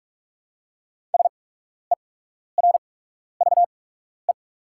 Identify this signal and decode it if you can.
Morse code, heard as a keyed tone, about 44 wpm, spaced as for 12 wpm, 690 Hz. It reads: SERVE
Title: 8x Character Spacing - Top 500 Words - 45wpm